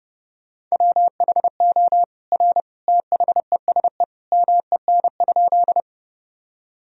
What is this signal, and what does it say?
Morse code, heard as a keyed tone, about 32 wpm, 700 Hz. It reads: WHO R THESE MEN?
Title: Sentences from Top 200 Words and Abbreviations 30wpm